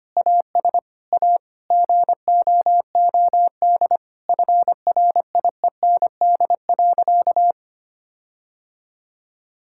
Morse code, heard as a keyed tone, about 25 wpm, 700 Hz. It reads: AS A GOOD FRIEND.
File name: practice / 25wpm / Sentences with 4-Word Phrases - 25wpm